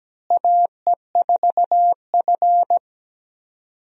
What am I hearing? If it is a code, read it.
Morse code, heard as a keyed tone, about 17 wpm, 700 Hz. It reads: AE4F
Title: USA Call Signs - 4 Characters - Repeated 3x - 17wpm